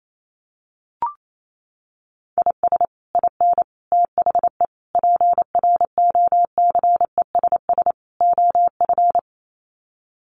Morse code, heard as a keyed tone, about 28 wpm, 700 Hz. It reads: IS IN THE PROCESS OF
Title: Sets of 5 Words Encore2 28wpm